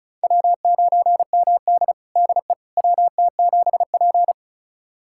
Morse code, heard as a keyed tone, about 36 wpm, 700 Hz. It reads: W9MD DE WT7P
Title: USA Callsign DE Callsign - 4 Characters - 35wpm